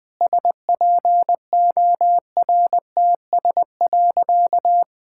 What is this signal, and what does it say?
Morse code, heard as a keyed tone, about 20 wpm, 700 Hz. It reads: SPORTS.
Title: Sentences with 6-Word Phrases - 20wpm